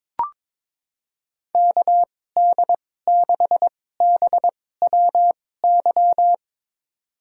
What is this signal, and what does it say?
Morse code, heard as a keyed tone, about 22 wpm, spaced as for 16 wpm, 700 Hz. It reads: KD6BWY
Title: USA Call Signs - 6 Characters - 2x Character Spacing - 22wpm